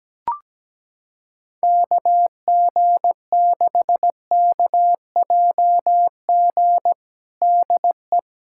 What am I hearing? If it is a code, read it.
Morse code, heard as a keyed tone, about 17 wpm, 700 Hz. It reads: KG6KJG DE
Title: USA Callsign DE Callsign - 6 Characters - 17wpm